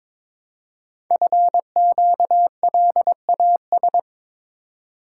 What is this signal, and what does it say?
Morse code, heard as a keyed tone, about 22 wpm, 700 Hz. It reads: FQLAS
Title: Code Groups - 5 Letters - 22wpm